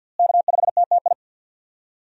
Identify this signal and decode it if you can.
Morse code, heard as a keyed tone, about 53 wpm, 700 Hz. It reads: K5TTI
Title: USA Call Signs - 5 Characters - Repeated 3x - 50wpm